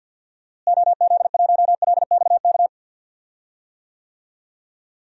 Morse code, heard as a keyed tone, about 50 wpm, 700 Hz. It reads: OZ1LXK
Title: DXCC-100 Call Signs - 6 Characters - Repeated 2x - 50wpm